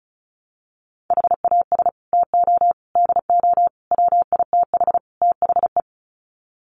Morse code, heard as a keyed tone, about 34 wpm, 700 Hz. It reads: HAS TO DO WITH THE
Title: Sets of 5 Words 35wpm